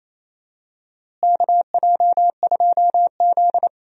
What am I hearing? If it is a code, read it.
Morse code, heard as a keyed tone, about 28 wpm, 700 Hz. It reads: KJ2Z
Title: USA Call Signs - 4 Characters - Repeated 3x - 28wpm